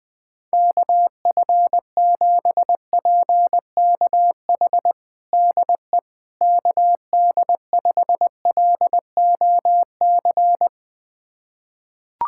Morse code, heard as a keyed tone, about 20 wpm, 700 Hz. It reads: KF7PKH DE KD5LOC
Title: USA Callsign DE Callsign 20wpm